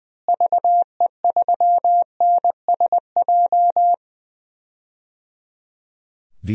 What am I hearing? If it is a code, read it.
Morse code, heard as a keyed tone, about 19 wpm, 700 Hz. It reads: VE3NSJ
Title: DXCC-100 Call Signs - 6 Characters - Repeated 3x - 20wpm